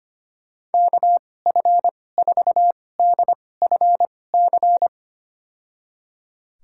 Morse code, heard as a keyed tone, about 25 wpm, spaced as for 18 wpm, 700 Hz. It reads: KF4DFC
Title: USA Call Signs - 6 Characters - 2x Character Spacing - 25wpm